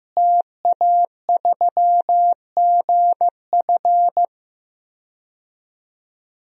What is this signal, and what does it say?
Morse code, heard as a keyed tone, about 15 wpm, 700 Hz. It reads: TA3GF
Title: DXCC-100 Call Signs - 5 Characters - Repeated 3x - 15wpm